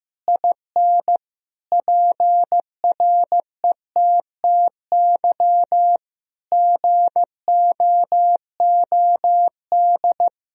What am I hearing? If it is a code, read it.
Morse code, heard as a keyed tone, about 15 wpm, 700 Hz. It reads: IN PRETTY GOOD
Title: Sentences from Top 1000 Words - Review Part 3 - 15wpm